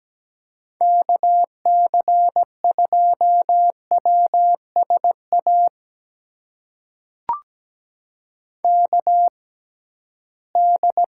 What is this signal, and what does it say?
Morse code, heard as a keyed tone, about 17 wpm, spaced as for 13 wpm, 700 Hz. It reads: KC2WSA KD
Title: USA Call Signs - 6 Characters - 6x Character Spacing - 17wpm